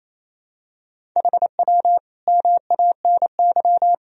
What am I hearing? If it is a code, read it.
Morse code, heard as a keyed tone, about 28 wpm, 700 Hz. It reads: HW MANY
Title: Sentences from Top 300 Words and Abbreviations 28wpm